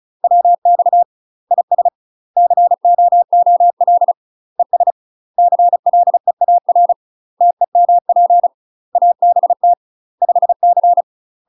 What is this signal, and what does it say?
Morse code, heard as a keyed tone, about 34 wpm, 700 Hz. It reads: WX IS COOL ES CLEAR TEMP ABT 5C
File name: UK MASTER QSOs TRAINING / 35WPM / UK MASTER QSOs 0004